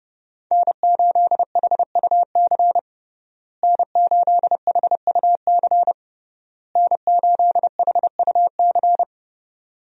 Morse code, heard as a keyed tone, about 30 wpm, 700 Hz. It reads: N8HUC N8HUC N8HUC
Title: USA Call Signs - 5 Characters - Repeated 3x - 30wpm